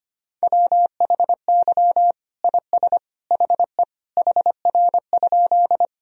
Morse code, heard as a keyed tone, about 25 wpm, 700 Hz. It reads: WHY IS HE HR?
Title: Sentences from Top 200 Words and Abbreviations 25wpm